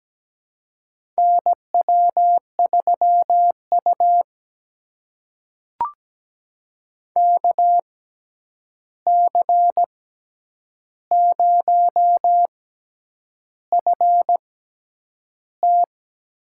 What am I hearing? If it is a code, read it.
Morse code, heard as a keyed tone, about 17 wpm, spaced as for 8 wpm, 700 Hz. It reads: NW3U KC0FT
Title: USA Call Signs - 6x Character Spacing - 17wpm